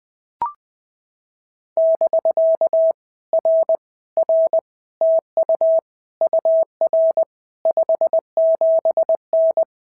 Morse code, heard as a keyed tone, about 19 wpm, 650 Hz. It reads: <BK> R R TU UR 57N
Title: POTA QSO Practice - 20wpm